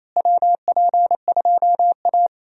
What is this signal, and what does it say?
Morse code, heard as a keyed tone, about 27 wpm, 700 Hz. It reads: WP2A
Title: USA Call Signs - 4 Characters - Repeated 3x - 28wpm